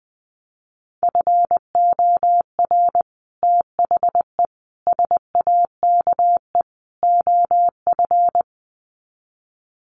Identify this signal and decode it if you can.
Morse code, heard as a keyed tone, about 19 wpm, 700 Hz. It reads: FOR THE SAKE OF
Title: Sets of 4 Words 20wpm